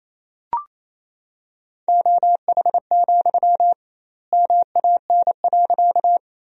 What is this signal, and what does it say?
Morse code, heard as a keyed tone, about 28 wpm, 700 Hz. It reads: OH, MAN.
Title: 2x Word-Spacing - Sentences from Top 500 Words - 28wpm